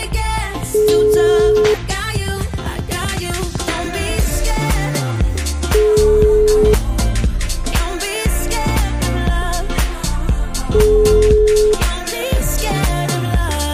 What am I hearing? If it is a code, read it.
Ringback tone (Europe, 425 Hz), with noise.